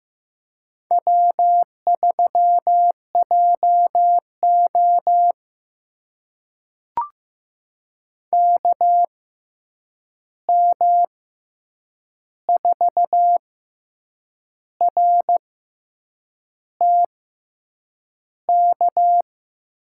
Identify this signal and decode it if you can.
Morse code, heard as a keyed tone, about 15 wpm, spaced as for 7 wpm, 700 Hz. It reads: W3JO KM4RTK
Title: USA Call Signs - 6x Character Spacing - 15wpm